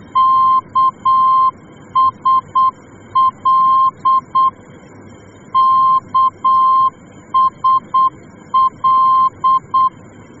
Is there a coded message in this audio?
Yes: Morse code from 0.2 to 9.9 s, heard as a keyed tone, about 8 wpm, 1050 Hz. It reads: KSL KSL